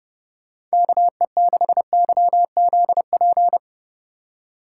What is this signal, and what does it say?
Morse code, heard as a keyed tone, about 36 wpm, 700 Hz. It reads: KE6YZP
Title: Speed Racing - USA 2x3 Call Signs 30wpm